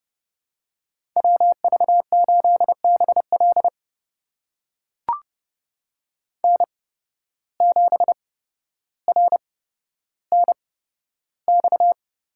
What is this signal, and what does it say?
Morse code, heard as a keyed tone, about 30 wpm, spaced as for 13 wpm, 700 Hz. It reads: WV8BL N7RNX
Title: USA Call Signs - 5 Characters - 8x Character Spacing - 30wpm